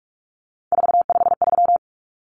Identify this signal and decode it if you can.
Morse code, heard as a keyed tone, about 44 wpm, 700 Hz. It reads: 453
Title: Common Cut Numbers - 3-Digit 45wpm